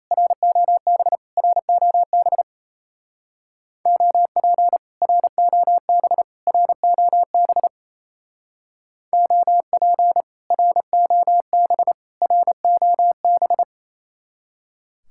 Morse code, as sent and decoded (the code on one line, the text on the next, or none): .-. --- -... / .-. --- -... / --- .--. / .-. --- -... / .-. --- -... / --- .--. / .-. --- -... / .-. --- -...
ROB ROB OP ROB ROB OP ROB ROB